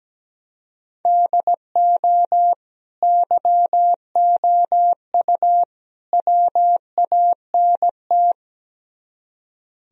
Morse code, heard as a keyed tone, about 17 wpm, 700 Hz. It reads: DO YOU WANT
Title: Speed-Racing - Sets of 3 Words 17wpm